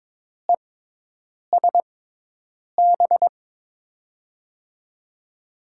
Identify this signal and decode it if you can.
Morse code, heard as a keyed tone, about 22 wpm, spaced as for 8 wpm, 700 Hz. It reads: ESB
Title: USA Call Signs - 6 Characters - 6x Character Spacing - 22wpm